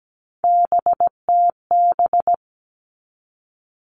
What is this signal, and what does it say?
Morse code, heard as a keyed tone, about 17 wpm, 700 Hz. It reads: BTB
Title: All Cut Numbers - 3-Digit 17wpm